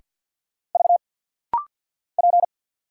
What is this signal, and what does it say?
Morse code, heard as a keyed tone, about 50 wpm, 700 Hz. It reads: V P
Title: Single Letter-Number Alt - Rapid-Fire 50wpm